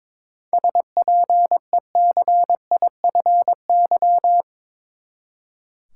Morse code, heard as a keyed tone, about 22 wpm, 700 Hz. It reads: SPECIFY
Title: Seven Letter Words - 22wpm